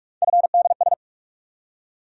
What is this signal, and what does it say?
Morse code, heard as a keyed tone, about 43 wpm, 700 Hz. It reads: FDS
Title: USA Call Sign Postfixes - 3 Letters 45wpm